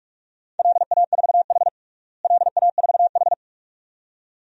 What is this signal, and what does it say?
Morse code, heard as a keyed tone, about 42 wpm, 700 Hz. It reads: LA4H LA4H
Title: DXCC-100 Call Signs - 4 Characters - Repeated 2x - 45wpm